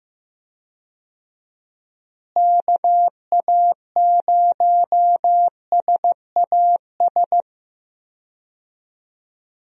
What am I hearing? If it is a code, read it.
Morse code, heard as a keyed tone, about 15 wpm, 700 Hz. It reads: KA0SAS
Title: USA Call Signs - 6 Characters - Repeated 3x - 15wpm